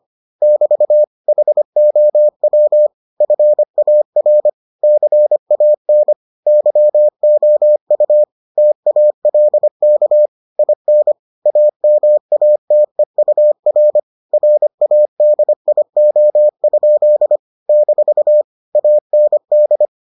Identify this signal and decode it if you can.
Morse code, heard as a keyed tone, about 25 wpm, 600 Hz. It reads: <BT> HOW FAR CAN YOU TALK IN AMATEUR RADIO? - AND